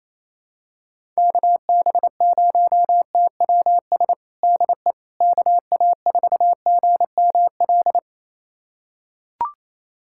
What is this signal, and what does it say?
Morse code, heard as a keyed tone, about 28 wpm, 700 Hz. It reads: KB0TWS DE KA4GML
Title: USA Callsign DE Callsign - 6 Characters - 28wpm